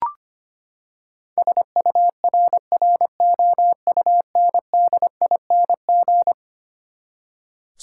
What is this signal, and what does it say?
Morse code, heard as a keyed tone, about 25 wpm, 700 Hz. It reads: SURROUNDING
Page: morse-code-ninja-practice-25wpm